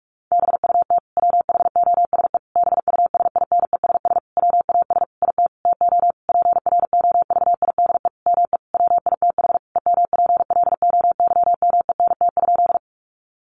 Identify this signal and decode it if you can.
Morse code, heard as a keyed tone, about 44 wpm, 700 Hz. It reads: BUT WHOSE BUSINESS WAS IT TO PROVIDE ME WITH EMPLOYMENT?
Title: Sentences from Top 1500 Words 45wpm